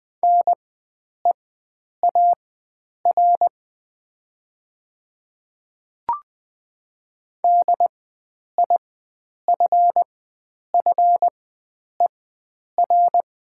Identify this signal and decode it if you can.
Morse code, heard as a keyed tone, about 20 wpm, spaced as for 9 wpm, 700 Hz. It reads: NEAR DIFFER
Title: 4x Character Spacing - Top 300 Words - 20wpm